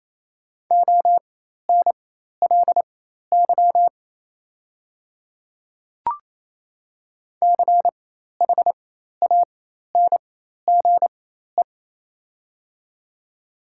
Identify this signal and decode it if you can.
Morse code, heard as a keyed tone, about 28 wpm, spaced as for 13 wpm, 690 Hz. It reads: ONLY CHANGE
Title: 4x Character Spacing - Top 300 Words - 28wpm